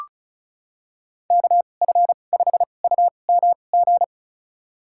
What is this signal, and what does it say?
Morse code, heard as a keyed tone, about 34 wpm, spaced as for 25 wpm, 700 Hz. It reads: KF5UMG
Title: USA Call Signs - 6 Characters - 2x Character Spacing - 35wpm